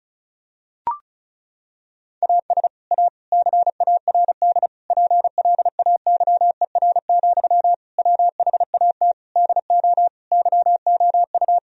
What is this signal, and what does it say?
Morse code, heard as a keyed tone, about 34 wpm, 700 Hz. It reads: AS A CARD PLAYER, WHAT DO YOU
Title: Sentences from Top 1000 Words - Review Part 4 - 35wpm